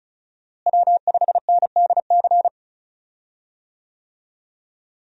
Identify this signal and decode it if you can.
Morse code, heard as a keyed tone, about 36 wpm, 700 Hz. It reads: W5NDC
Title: USA Call Signs - 5 Characters - Repeated 3x - 35wpm